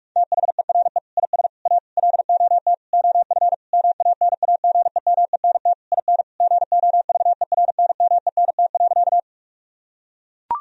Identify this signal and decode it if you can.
Morse code, heard as a keyed tone, about 41 wpm, 700 Hz. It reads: THERE IS A LOT OF MANAGEMENT IN GOVERNMENT.
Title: Sentences from Top 1000 Words - Review Part 4 - 45wpm